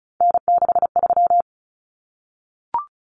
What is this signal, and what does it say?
Morse code, heard as a keyed tone, about 34 wpm, 700 Hz. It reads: N63
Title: Common Cut Numbers - 3-Digit 35wpm